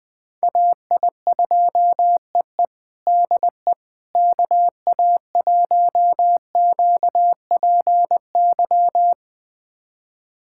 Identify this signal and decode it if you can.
Morse code, heard as a keyed tone, about 20 wpm, 700 Hz. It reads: AI2EE DE KA1QPY
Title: USA Callsign DE Callsign 20wpm